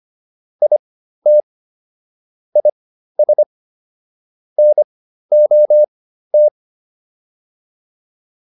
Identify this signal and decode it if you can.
Morse code, heard as a keyed tone, about 25 wpm, spaced as for 13 wpm, 600 Hz. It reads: IT IS NOT